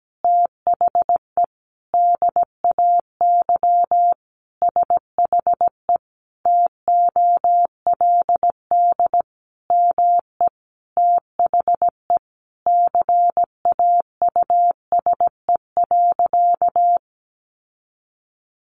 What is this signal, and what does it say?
Morse code, heard as a keyed tone, about 17 wpm, 700 Hz. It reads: THE DAY SHE TOLD ME THE CAUSE.
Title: Sentences from Top 700 Words 17wpm